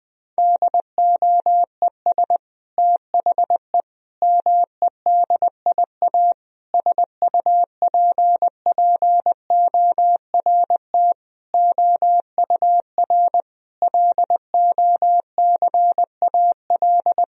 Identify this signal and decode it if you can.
Morse code, heard as a keyed tone, about 20 wpm, 700 Hz. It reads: DOES THE MEDIA SUPPORT OUR LOCAL
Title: Sentences from Top 1000 Words - Review Part 4 - 20wpm